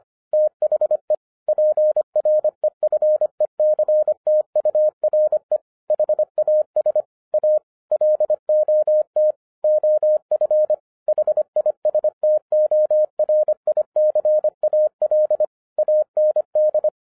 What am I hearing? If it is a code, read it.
Morse code, heard as a keyed tone, about 26 wpm, 610 Hz. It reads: THE PREFECTURE HAS A LOT OF HISTORICAL AND
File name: 2011-10-mp3 / n04-JUST MOVED 25wpm